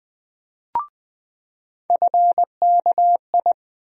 Morse code, heard as a keyed tone, about 20 wpm, 700 Hz. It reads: FKI
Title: USA Call Sign Postfixes - 3 Letters 20wpm